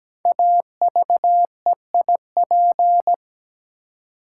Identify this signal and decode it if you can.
Morse code, heard as a keyed tone, about 17 wpm, 700 Hz. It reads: AVEIP
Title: Code Groups - 5 Letters - 17wpm